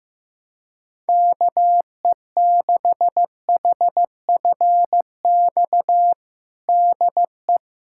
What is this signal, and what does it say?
Morse code, heard as a keyed tone, about 15 wpm, 700 Hz. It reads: KE6HFX DE
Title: USA Callsign DE Callsign - 6 Characters - 15wpm